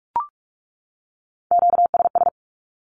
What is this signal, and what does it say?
Morse code, heard as a keyed tone, about 44 wpm, 700 Hz. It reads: QSS
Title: Q-Codes 45wpm